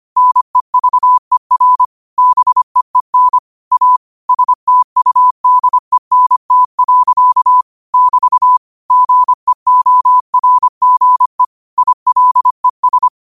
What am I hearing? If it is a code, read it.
Morse code, heard as a keyed tone, about 25 wpm, 1000 Hz. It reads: NEVER BEEN A STUDENT. <BT> GEORGE ILES
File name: qotdfiles / 25wpm